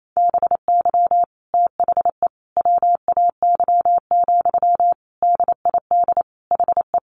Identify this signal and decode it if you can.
Morse code, heard as a keyed tone, about 28 wpm, 700 Hz. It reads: BY THE WAY, DID HE
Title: Sentences from Top 1300 Words 28wpm